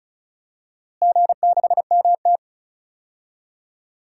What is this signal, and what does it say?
Morse code, heard as a keyed tone, about 36 wpm, 700 Hz. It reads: G6MT
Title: DXCC-100 Call Signs - 4 Characters - Repeated 3x - 35wpm